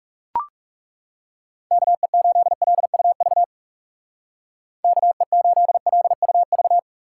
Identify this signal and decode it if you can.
Morse code, heard as a keyed tone, about 36 wpm, 700 Hz. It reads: KE8LUV KE8LUV
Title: Speed Racing - USA 2x3 Call Signs 30wpm